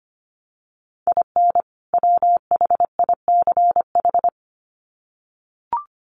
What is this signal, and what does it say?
Morse code, heard as a keyed tone, about 25 wpm, 700 Hz. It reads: IN WHICH
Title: Sets of 2 Words 25wpm